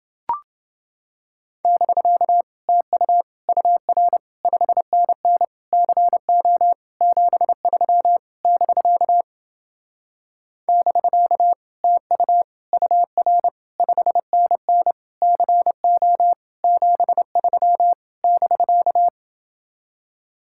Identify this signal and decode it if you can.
Morse code, heard as a keyed tone, about 29 wpm, 700 Hz. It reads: <BK> TU UR 5NN CO 73 <BK> <BK> TU UR 5NN CO 73 <BK>